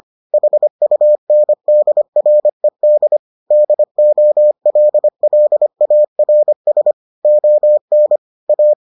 Morse code, heard as a keyed tone, about 25 wpm, 600 Hz. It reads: HUNDRED DOLLARS ON A